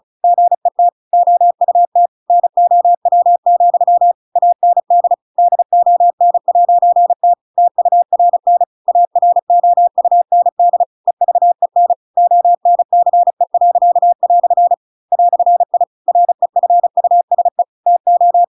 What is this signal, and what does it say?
Morse code, heard as a keyed tone, about 34 wpm, 700 Hz. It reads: GET OUT NOW, AND DON'T TURN AROUND EVEN ONCE." "I REFUSE TO